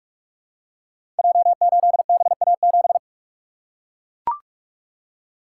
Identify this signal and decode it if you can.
Morse code, heard as a keyed tone, about 44 wpm, 700 Hz. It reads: J8BA7
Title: Code Groups - 5 Letter:Numbers - 45wpm